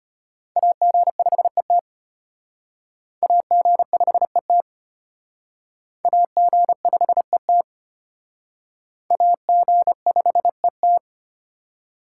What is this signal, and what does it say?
Morse code, heard as a keyed tone, about 31 wpm, 700 Hz. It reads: AG5ET AG5ET AG5ET AG5ET
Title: Speed Racing - USA 2x2 Call Signs 25wpm